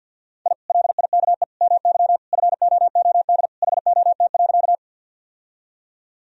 Morse code, heard as a keyed tone, about 46 wpm, 700 Hz. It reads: I LIKE MY FOOD HOT.
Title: Sentences from Top 1000 Words - Review Part 3 - 50wpm